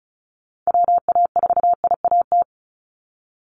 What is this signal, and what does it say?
Morse code, heard as a keyed tone, about 34 wpm, 700 Hz. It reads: WA4IAT
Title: USA Call Signs 35wpm